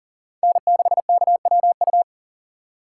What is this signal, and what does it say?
Morse code, heard as a keyed tone, about 38 wpm, 700 Hz. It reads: N6KWU
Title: USA Call Signs - 2x Character Spacing - 40wpm